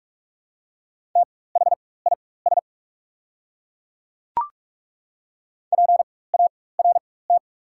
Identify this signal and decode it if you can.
Morse code, heard as a keyed tone, about 48 wpm, spaced as for 21 wpm, 690 Hz. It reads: THIS PART
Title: 4x Character Spacing - Top 200 Words - 45wpm